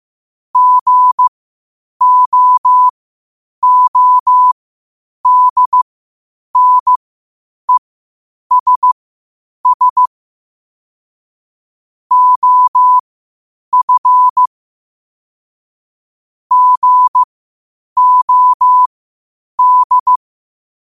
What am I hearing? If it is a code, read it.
Morse code, heard as a keyed tone, about 15 wpm, spaced as for 8 wpm, 1000 Hz. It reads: GOODNESS OF GOD
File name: qotdfiles / 8wpm